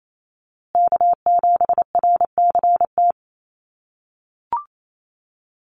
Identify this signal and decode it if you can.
Morse code, heard as a keyed tone, about 28 wpm, 700 Hz. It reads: K7RCT
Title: USA Call Signs Repeated 3x 28wpm